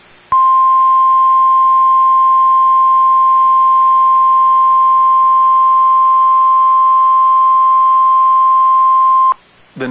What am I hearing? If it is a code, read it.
Emergency Alert System (SAME) tones (the 1050 Hz attention tone). It is heard from 0.3 to 9.4 s.